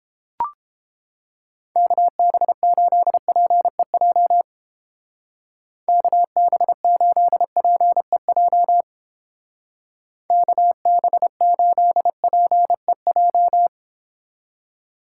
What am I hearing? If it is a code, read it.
Morse code, heard as a keyed tone, about 31 wpm, 700 Hz. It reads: KB8PEJ KB8PEJ KB8PEJ